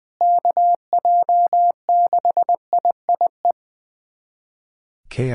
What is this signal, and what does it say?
Morse code, heard as a keyed tone, about 20 wpm, 700 Hz. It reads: KJ6IIE
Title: USA Callsign DE Callsign - 6 Characters - 20wpm